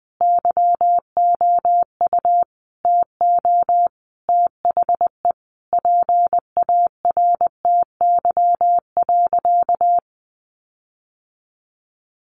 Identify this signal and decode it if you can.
Morse code, heard as a keyed tone, about 20 wpm, 700 Hz. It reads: YOU TO THE PARTY.
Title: Sentences from Top 1500 Words 20wpm